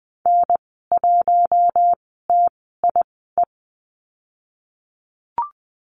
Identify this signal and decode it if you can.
Morse code, heard as a keyed tone, about 20 wpm, spaced as for 14 wpm, 700 Hz. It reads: N1TIE
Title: 154 Lesson 25g - Letter p - Call Signs 20wpm